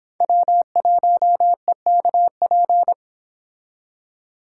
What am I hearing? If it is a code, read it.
Morse code, heard as a keyed tone, about 32 wpm, 700 Hz. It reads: W1EKP